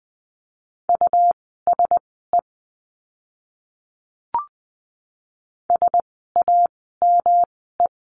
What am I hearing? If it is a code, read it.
Morse code, heard as a keyed tone, about 20 wpm, spaced as for 15 wpm, 700 Hz. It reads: USE SAME
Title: 146 Lesson 24f - Letter y - Words 20wpm